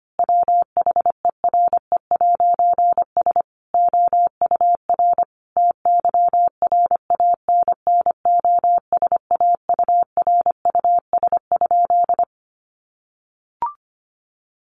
Morse code, heard as a keyed tone, about 25 wpm, 700 Hz. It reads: WHERE'S OUR TYRANNOSAURUS?
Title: A Sound Of Thunder 25wpm